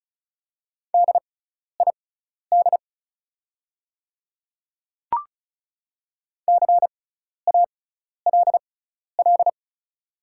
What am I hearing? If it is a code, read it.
Morse code, heard as a keyed tone, about 36 wpm, spaced as for 12 wpm, 690 Hz. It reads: DID CALL
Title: 6x Character Spacing - Top 100 Words - 35wpm